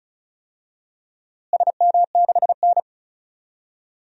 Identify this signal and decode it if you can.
Morse code, heard as a keyed tone, about 36 wpm, 700 Hz. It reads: SM6N
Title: DXCC-100 Call Signs - 4 Characters - Repeated 3x - 35wpm